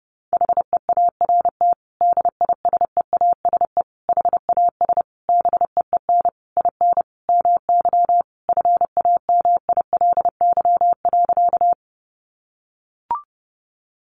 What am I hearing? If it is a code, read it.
Morse code, heard as a keyed tone, about 30 wpm, 700 Hz. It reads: HEART DISEASE HAS BEEN IN MY FAMILY.
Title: Sentences from Top 1200 Words 30wpm